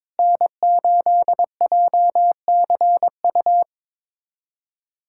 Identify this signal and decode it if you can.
Morse code, heard as a keyed tone, about 22 wpm, 700 Hz. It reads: N8JCU
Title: USA Callsign DE Callsign - 5 Characters - 22wpm